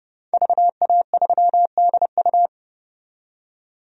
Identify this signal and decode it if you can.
Morse code, heard as a keyed tone, about 30 wpm, 700 Hz. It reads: VA3DU
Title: DXCC-100 Call Signs - 5 Characters - 30wpm